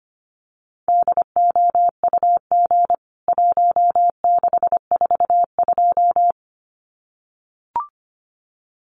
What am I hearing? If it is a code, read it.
Morse code, heard as a keyed tone, about 25 wpm, 700 Hz. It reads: DOUG 1642
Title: CWT Exchange 25wpm